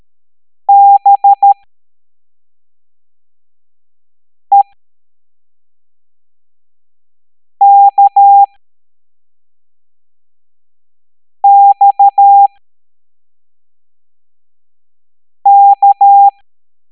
Morse code, heard as a keyed tone, about 13 wpm, spaced as for 3 wpm, 800 Hz. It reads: BEKXK
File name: hamdata / L07-EX03-HQP